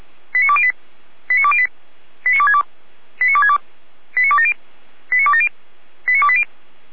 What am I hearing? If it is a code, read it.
Five-tone selective call (ZVEI), heard as quick runs of five tones, tones of about 70 ms. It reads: 89208 89208 89262 89262 89270 89270 89280